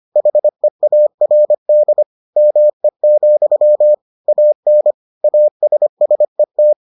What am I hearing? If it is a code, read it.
Morse code, heard as a keyed tone, about 25 wpm, 600 Hz. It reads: HEARD ME, AN ASSET